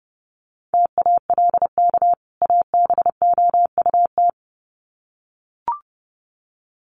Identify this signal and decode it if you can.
Morse code, heard as a keyed tone, about 30 wpm, 700 Hz. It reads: TALK ABOUT
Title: Sets of 2 Words 30wpm